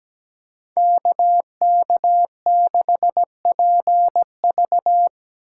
Morse code, heard as a keyed tone, about 19 wpm, 700 Hz. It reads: KK6PV
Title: Speed Racing - USA 2x2 Call Signs 17wpm